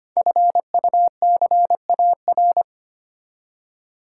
Morse code, heard as a keyed tone, about 25 wpm, 700 Hz. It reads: FUCAR
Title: Code Groups - 5 Letter:Numbers - 25wpm